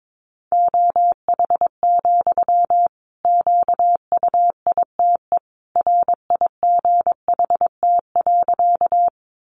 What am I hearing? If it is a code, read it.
Morse code, heard as a keyed tone, about 22 wpm, 700 Hz. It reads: OH, QUITE RIGHT.
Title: Sentences from Top 700 Words 22wpm